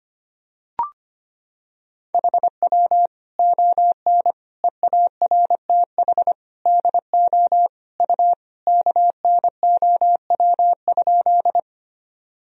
Morse code, heard as a keyed tone, about 25 wpm, 700 Hz. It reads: HW ON EARTH DO U KNOW?
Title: Sentences from Top 200 Words and Abbreviations 25wpm